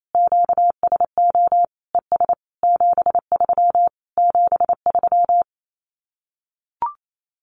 Morse code, heard as a keyed tone, about 28 wpm, 700 Hz. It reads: QSO ES 73 73
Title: Common QSO Phrases Encore 28wpm